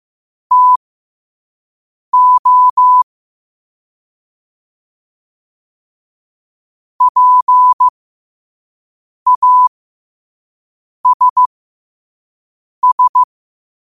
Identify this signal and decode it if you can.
Morse code, heard as a keyed tone, about 15 wpm, spaced as for 5 wpm, 1000 Hz. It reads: TO PASS
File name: qotdfiles / 5wpm